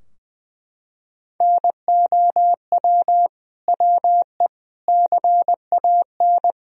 Morse code, heard as a keyed tone, about 20 wpm, 700 Hz. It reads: NOW WE CAN
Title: Sentences from Top 1000 Words - Review Part 4 - 20wpm